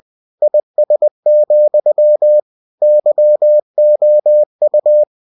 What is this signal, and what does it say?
Morse code, heard as a keyed tone, about 20 wpm, 600 Hz. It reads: IS, YOU
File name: cw-books / winnie-the-pooh / winnie-20wpm-page-0058